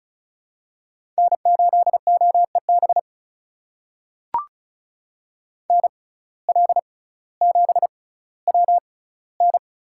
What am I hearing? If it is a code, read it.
Morse code, heard as a keyed tone, about 36 wpm, spaced as for 18 wpm, 700 Hz. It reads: N8OEB NL7WN
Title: USA Call Signs - 5 Characters - 6x Character Spacing - 35wpm